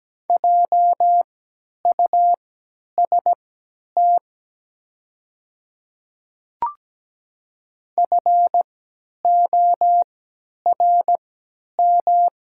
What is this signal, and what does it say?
Morse code, heard as a keyed tone, about 17 wpm, spaced as for 10 wpm, 690 Hz. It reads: JUST FORM
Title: 3x Character Spacing - Top 200 Words - 17wpm